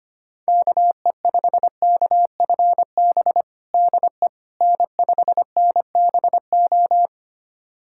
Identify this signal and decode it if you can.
Morse code, heard as a keyed tone, about 25 wpm, 700 Hz. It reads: KE5KFB DE N5NBO
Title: USA Callsign DE Callsign 25wpm